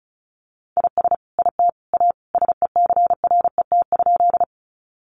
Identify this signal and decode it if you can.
Morse code, heard as a keyed tone, about 34 wpm, 700 Hz. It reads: IS IT A SECRET?
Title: Sentences from Top 1200 Words 35wpm